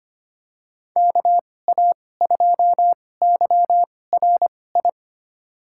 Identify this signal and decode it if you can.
Morse code, heard as a keyed tone, about 25 wpm, spaced as for 18 wpm, 700 Hz. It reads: KA2YRI
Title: USA Call Signs - 6 Characters - 2x Character Spacing - 25wpm